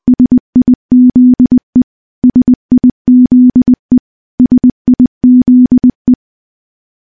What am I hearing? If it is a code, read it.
Morse code, heard as a keyed tone, about 21 wpm, 250 Hz. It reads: SIZE SIZE SIZE
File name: morse-training-20WPM